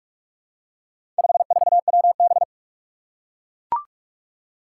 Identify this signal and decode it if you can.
Morse code, heard as a keyed tone, about 41 wpm, 700 Hz. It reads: 54WB
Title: Four Challenging Letter-Number Combinations 45wpm